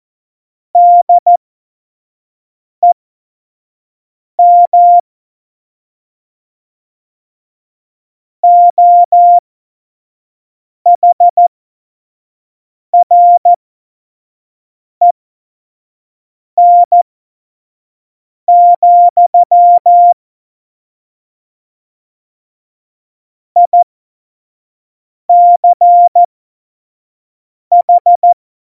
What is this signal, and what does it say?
Morse code, heard as a keyed tone, about 14 wpm, spaced as for 5 wpm, 700 Hz. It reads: DEM OHREN, ICH
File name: Morsekurs / Witze / Witz 21. November 2017, 25 F70